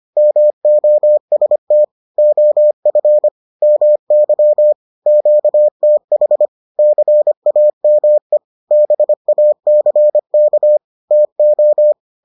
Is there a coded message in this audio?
Yes: Morse code from 0.2 to 11.9 s, heard as a keyed tone, about 25 wpm, 600 Hz. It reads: MOST OF MY QTH CAME BACK TO